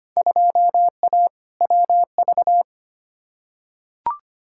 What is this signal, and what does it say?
Morse code, heard as a keyed tone, about 25 wpm, 700 Hz. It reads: 2A WV
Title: Field Day Exchanges Encore - 25wpm